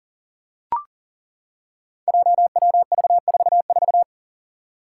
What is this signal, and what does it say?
Morse code, heard as a keyed tone, about 40 wpm, 700 Hz. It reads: JWV44
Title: Five Challenging Letter-Number Combinations 40wpm